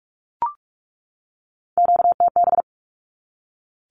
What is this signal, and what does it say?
Morse code, heard as a keyed tone, about 44 wpm, 700 Hz. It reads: QTB
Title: Q-Codes 45wpm